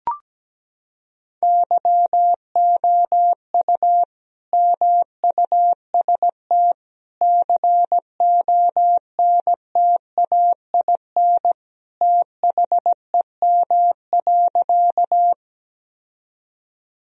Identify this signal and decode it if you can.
Morse code, heard as a keyed tone, about 17 wpm, 700 Hz. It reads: YOU MUST CONTAIN THEM.
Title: Sentences from Top 1000 Words - Review Part 2 - 17wpm